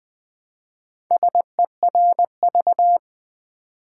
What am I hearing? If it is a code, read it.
Morse code, heard as a keyed tone, about 20 wpm, 700 Hz. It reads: SERV
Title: Build A Word - 1 Letter At A Time - 5-Letter Words - 20wpm